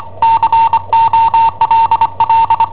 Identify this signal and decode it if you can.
Morse code, heard as a keyed tone, about 25 wpm, 910 Hz. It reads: COLL
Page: Cassetta esercizi preparata da I7OHP per IW5ELP